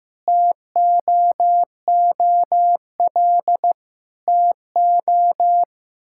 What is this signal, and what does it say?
Morse code, heard as a keyed tone, about 15 wpm, 700 Hz. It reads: TOOL TO
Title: Sentences from Top 1000 Words - Review Part 6 - 15wpm